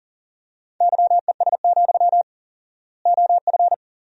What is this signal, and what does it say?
Morse code, heard as a keyed tone, about 37 wpm, 700 Hz. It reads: YES, OF
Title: 4x Word-Spacing - Sentences from Top 500 Words - 40wpm